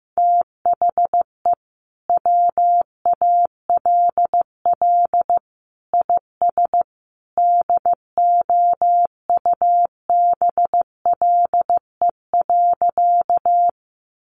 Morse code, heard as a keyed tone, about 15 wpm, 700 Hz. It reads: THE WALL IS DOUBLE.
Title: Sentences from Top 1400 Words 15wpm